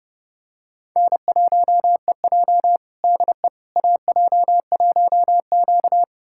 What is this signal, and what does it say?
Morse code, heard as a keyed tone, about 30 wpm, 700 Hz. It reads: N1EJ DE AJ1Q
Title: USA Callsign DE Callsign - 4 Characters - 30wpm